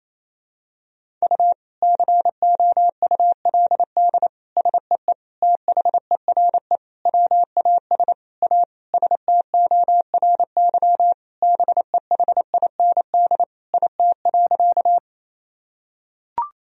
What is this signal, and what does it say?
Morse code, heard as a keyed tone, about 28 wpm, 700 Hz. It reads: U COULD SEE THERE WAS A STORY BEHIND IT.
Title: Sentences from Top 500 Words and Abbreviations 28wpm